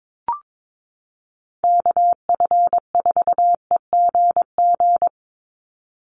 Morse code, heard as a keyed tone, about 22 wpm, 700 Hz. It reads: KF4EGG
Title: USA Call Signs 22wpm